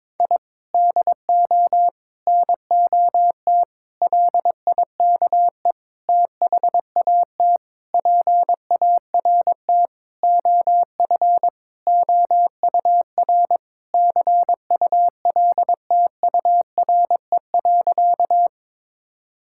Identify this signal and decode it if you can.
Morse code, heard as a keyed tone, about 22 wpm, 700 Hz. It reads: I DO NOT LIKE THAT PART OF OUR CULTURE.
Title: Sentences from Top 1000 Words - Review Part 6 - 22wpm